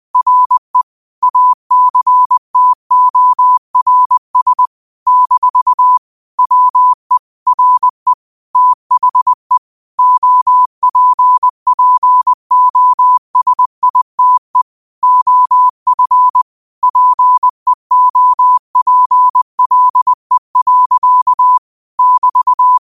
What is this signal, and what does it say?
Morse code, heard as a keyed tone, about 20 wpm, 1000 Hz. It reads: RE ACTORS - WE RE THE OPPOSITE OF PEOPLE. <BT>
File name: qotdfiles / 20wpm